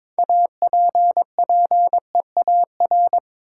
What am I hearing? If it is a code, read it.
Morse code, heard as a keyed tone, about 22 wpm, 700 Hz. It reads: APPEAR
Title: Sentences from Top 1200 Words - New Word Drill Alt - 22wpm